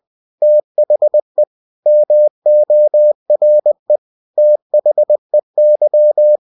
Morse code, heard as a keyed tone, about 20 wpm, 600 Hz. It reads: THE MORE THEY
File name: cw-books / winnie-the-pooh / winnie-20wpm-page-0094